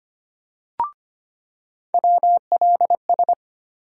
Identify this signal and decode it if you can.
Morse code, heard as a keyed tone, about 25 wpm, 700 Hz. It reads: WLS
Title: USA Call Sign Postfixes - 3 Letters 25wpm